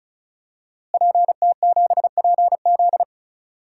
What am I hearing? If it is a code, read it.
Morse code, heard as a keyed tone, about 34 wpm, 700 Hz. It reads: PT7PZ
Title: DXCC-100 Call Signs - 5 Characters - Repeated 2x - 35wpm